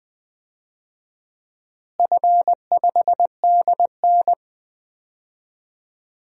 Morse code, heard as a keyed tone, about 21 wpm, 700 Hz. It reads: F5DN
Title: DXCC-100 Call Signs - 4 Characters - Repeated 2x - 20wpm